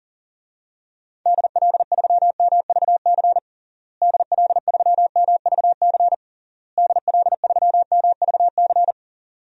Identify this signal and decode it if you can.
Morse code, heard as a keyed tone, about 40 wpm, 700 Hz. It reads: DL3MVC DL3MVC DL3MVC
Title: DXCC-100 Call Signs - 6 Characters - Repeated 3x - 40wpm